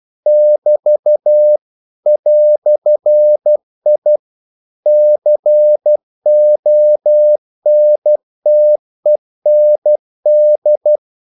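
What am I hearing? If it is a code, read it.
Morse code, heard as a keyed tone, about 12 wpm, 600 Hz. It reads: <BT> "I CONTEND
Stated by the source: Morse practice sending English quotations.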